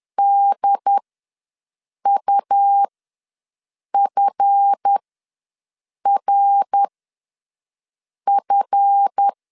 Morse code, heard as a keyed tone, about 10 wpm, spaced as for 5 wpm, 800 Hz. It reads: DUFRF
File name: LEZIONI / 7 GF9Z / 7 GF ws 10wpm 3918